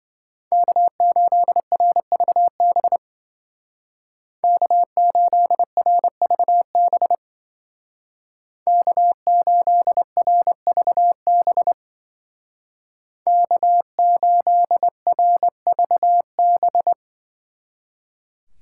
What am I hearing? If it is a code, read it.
Morse code, heard as a keyed tone, about 26 wpm, 700 Hz. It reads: K8RVB K8RVB K8RVB K8RVB